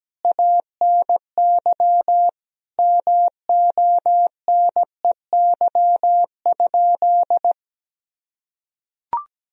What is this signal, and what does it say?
Morse code, heard as a keyed tone, about 17 wpm, 700 Hz. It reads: ANY MONEY?
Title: Sentences from Top 400 Words and Abbreviations 17wpm